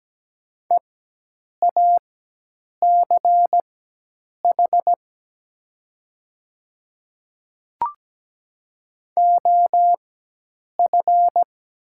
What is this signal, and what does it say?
Morse code, heard as a keyed tone, about 17 wpm, spaced as for 8 wpm, 700 Hz. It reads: EACH OF
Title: 4x Character Spacing - Top 100 Words - 17wpm